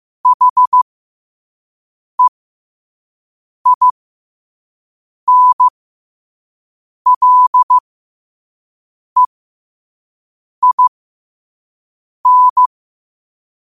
Morse code, heard as a keyed tone, about 15 wpm, spaced as for 5 wpm, 1000 Hz. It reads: HEINLEIN
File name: qotdfiles / 5wpm